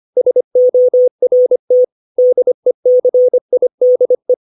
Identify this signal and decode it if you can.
Morse code, heard as a keyed tone, about 25 wpm, 500 Hz. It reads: SORT DECIDE
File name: 632 words / 632 Words 25WPM